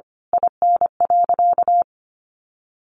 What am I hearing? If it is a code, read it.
Morse code, heard as a keyed tone, about 25 wpm, 700 Hz. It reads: IN.
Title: Sentences from Top 1300 Words 25wpm